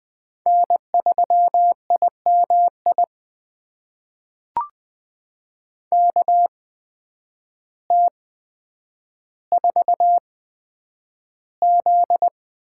Morse code, heard as a keyed tone, about 20 wpm, spaced as for 9 wpm, 700 Hz. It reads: N3IMI KT4Z
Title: USA Call Signs - 5 Characters - 8x Character Spacing - 20wpm